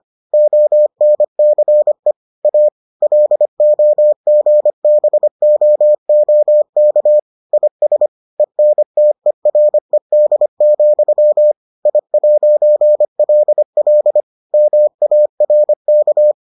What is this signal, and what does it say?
Morse code, heard as a keyed tone, about 25 wpm, 600 Hz. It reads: ONCE A LOGBOOK IS ENTERED, I'LL MARK